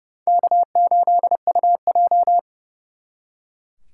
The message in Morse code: K8UJ

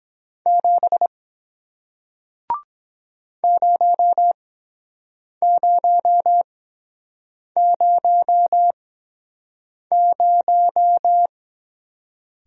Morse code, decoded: 7 0000

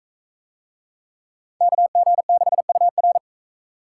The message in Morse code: KG6UR

